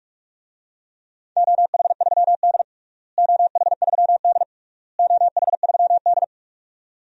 Morse code, heard as a keyed tone, about 46 wpm, 700 Hz. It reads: OH3D OH3D OH3D
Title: DXCC-100 Call Signs - 4 Characters - Repeated 3x - 45wpm